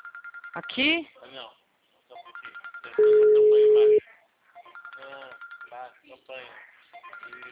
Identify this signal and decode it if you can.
Ringback tone (Europe, 425 Hz), with speech.